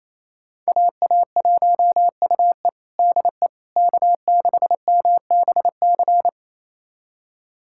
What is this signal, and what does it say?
Morse code, heard as a keyed tone, about 28 wpm, 700 Hz. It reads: AA1UE DE K6MBC
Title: USA Callsign DE Callsign - 5 Characters - 28wpm